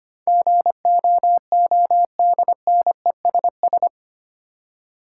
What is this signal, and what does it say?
Morse code, heard as a keyed tone, about 25 wpm, 700 Hz. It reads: GOODNESS
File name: Eight Letter Words - 25wpm